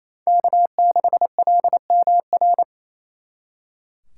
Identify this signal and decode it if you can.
Morse code, heard as a keyed tone, about 35 wpm, 700 Hz. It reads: K6LMR
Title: Speed Racing - USA Call Signs 28wpm